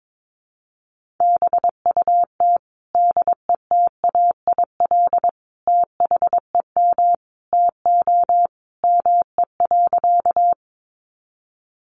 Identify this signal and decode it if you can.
Morse code, heard as a keyed tone, about 22 wpm, 700 Hz. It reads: BUT DETAIL THEM TO ME.
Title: Sentences from Top 1500 Words 22wpm